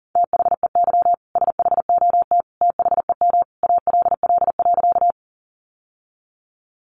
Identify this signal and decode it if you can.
Morse code, heard as a keyed tone, about 40 wpm, 700 Hz. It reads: THEY SHOT THEM ALL.
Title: Sentences from Top 1100 Words 40wpm